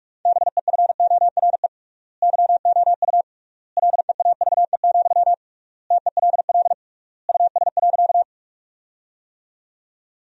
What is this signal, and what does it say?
Morse code, heard as a keyed tone, about 43 wpm, 700 Hz. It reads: BEFORE YOU LEAVE, TELL US.
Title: 3x Word-Spacing - Sentences from Top 400 Words - 45wpm